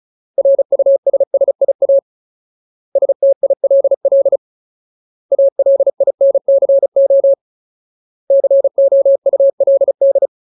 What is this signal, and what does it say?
Morse code, heard as a keyed tone, about 34 wpm, 550 Hz. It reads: RUSSIA STILL ALINCO COULD
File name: Ultimate Word List / 35WPM Ultimate